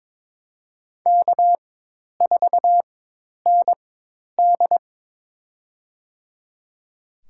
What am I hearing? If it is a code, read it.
Morse code, heard as a keyed tone, about 22 wpm, spaced as for 10 wpm, 700 Hz. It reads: K4ND
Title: USA Call Signs - 4 Characters - 4x Character Spacing - 22wpm